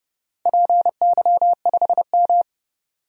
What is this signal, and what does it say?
Morse code, heard as a keyed tone, about 32 wpm, 700 Hz. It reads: PY5M
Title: DXCC-100 Call Signs - 4 Characters - 30wpm